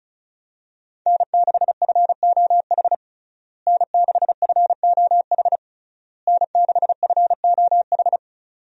Morse code, heard as a keyed tone, about 36 wpm, 700 Hz. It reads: N6FOH N6FOH N6FOH
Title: USA Call Signs - 5 Characters - Repeated 3x - 35wpm